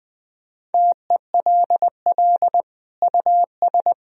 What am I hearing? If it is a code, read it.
Morse code, heard as a keyed tone, about 21 wpm, 700 Hz. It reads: TELL US
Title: Sentences from Top 1000 Words - Review Part 4 - 20wpm